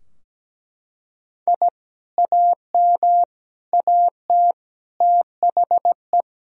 Morse code, heard as a keyed tone, about 17 wpm, 700 Hz. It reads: I AM AT THE
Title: Sentences from Top 1000 Words - Review Part 2 - 17wpm